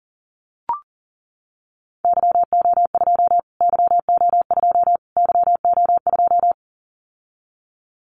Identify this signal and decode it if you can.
Morse code, heard as a keyed tone, about 40 wpm, 700 Hz. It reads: YO2 YO2 YO2
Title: Callsign Prefixes for Top 25 DXCC - Repeated 3x - 40wpm